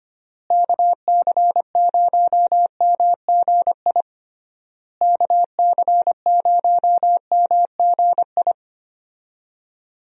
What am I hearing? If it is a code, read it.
Morse code, heard as a keyed tone, about 25 wpm, 700 Hz. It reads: KC0MGI KC0MGI
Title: USA Call Signs - 6 Characters - Repeated 3x - 25wpm